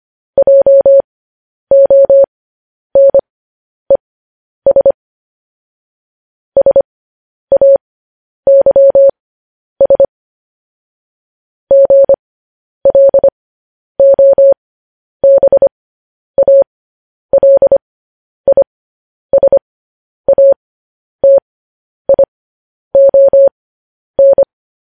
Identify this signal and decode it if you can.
Morse code, heard as a keyed tone, about 25 wpm, spaced as for 10 wpm, 550 Hz. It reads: JONES SAYS GLOBALISATION
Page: news headlines in morse code at 10 wpm